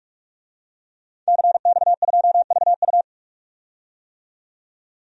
Morse code, heard as a keyed tone, about 47 wpm, 700 Hz. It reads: CX2VU
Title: DXCC-100 Call Signs - 5 Characters - 45wpm